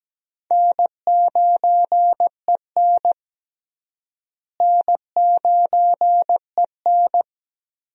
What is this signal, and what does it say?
Morse code, heard as a keyed tone, about 17 wpm, 700 Hz. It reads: N9EN N9EN
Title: USA Call Signs - 4 Characters - Repeated 3x - 17wpm